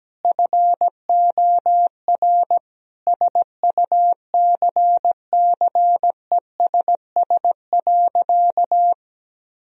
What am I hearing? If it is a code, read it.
Morse code, heard as a keyed tone, about 17 wpm, 700 Hz. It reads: FOR SUCCESS.
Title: Sentences from Top 1000 Words - Review Part 4 - 17wpm